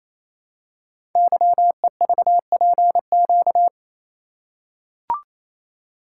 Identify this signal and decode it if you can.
Morse code, heard as a keyed tone, about 27 wpm, 690 Hz. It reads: YEVPQ
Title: Code Groups - 5 Letters - 28wpm